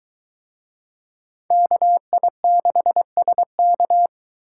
Morse code, heard as a keyed tone, about 23 wpm, 700 Hz. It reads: KI6SK